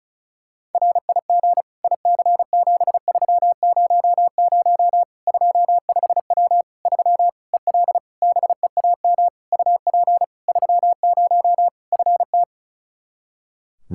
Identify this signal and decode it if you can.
Morse code, heard as a keyed tone, about 34 wpm, 700 Hz. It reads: RIG IC7300 25W 3 EL BEAM UP 30 FT
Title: Practice QSO - Exchange 2 - Info - 35wpm